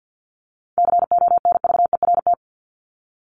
Morse code, heard as a keyed tone, about 49 wpm, 700 Hz. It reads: CONVERT
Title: Top 1751-2000 Words 50wpm